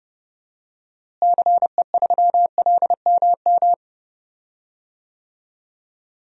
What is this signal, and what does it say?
Morse code, heard as a keyed tone, about 30 wpm, 700 Hz. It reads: CE3LMM
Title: DXCC-100 Call Signs - 6 Characters - Repeated 3x - 30wpm